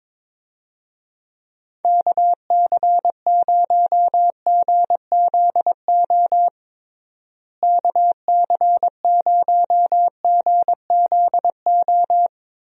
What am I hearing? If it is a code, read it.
Morse code, heard as a keyed tone, about 22 wpm, 700 Hz. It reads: KC0GZO KC0GZO
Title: USA Call Signs - 6 Characters - Repeated 2x - 22wpm